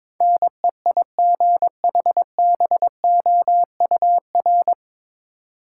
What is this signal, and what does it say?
Morse code, heard as a keyed tone, about 22 wpm, 700 Hz. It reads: NEIGHBOUR
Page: Morse Code Ninja practice 22wpm